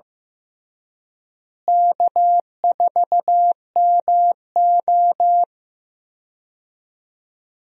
Morse code, heard as a keyed tone, about 15 wpm, 700 Hz. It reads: K4MO
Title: USA Call Signs - 4 Characters - Repeated 3x - 15wpm